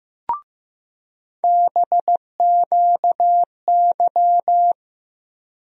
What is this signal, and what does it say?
Morse code, heard as a keyed tone, about 15 wpm, 700 Hz. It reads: BQY
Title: USA Call Sign Postfixes - 3 Letters 15wpm